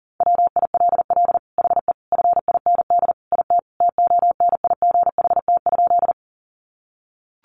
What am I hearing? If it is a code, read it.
Morse code, heard as a keyed tone, about 37 wpm, 700 Hz. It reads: WILL HE FIND IT TONIGHT?
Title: Sentences from Top 600 Words 40wpm